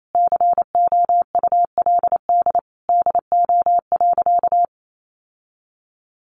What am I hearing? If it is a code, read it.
Morse code, heard as a keyed tone, about 28 wpm, 700 Hz. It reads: COULD DO.
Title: Sentences from Top 1400 Words 28wpm